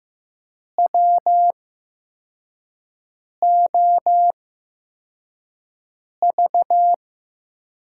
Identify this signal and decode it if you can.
Morse code, heard as a keyed tone, about 15 wpm, spaced as for 4 wpm, 700 Hz. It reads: WOV
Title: USA Call Signs - 5 Characters - 8x Character Spacing - 15wpm